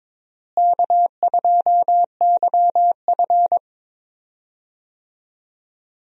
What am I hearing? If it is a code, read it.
Morse code, heard as a keyed tone, about 22 wpm, 700 Hz. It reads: K2YF
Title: USA Call Signs - 4 Characters - Repeated 2x - 22wpm